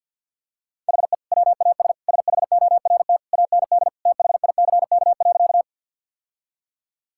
Morse code, heard as a keyed tone, about 50 wpm, 700 Hz. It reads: HE WAS SHORT AND THICK.
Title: Sentences from Top 1600 Words 50wpm